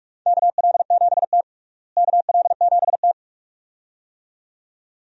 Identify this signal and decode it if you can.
Morse code, heard as a keyed tone, about 46 wpm, 700 Hz. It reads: KL7T KL7T
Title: USA Call Signs - 4 Characters - Repeated 2x - 45wpm